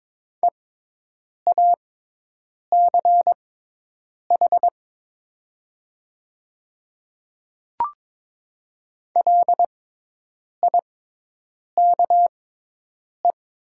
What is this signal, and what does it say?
Morse code, heard as a keyed tone, about 22 wpm, spaced as for 8 wpm, 700 Hz. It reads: EACH LIKE
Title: 6x Character Spacing - Top 100 Words - 22wpm